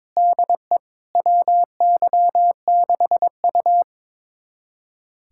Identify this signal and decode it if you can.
Morse code, heard as a keyed tone, about 22 wpm, 700 Hz. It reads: DE WY6U
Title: USA Callsign DE Callsign - 4 Characters - 22wpm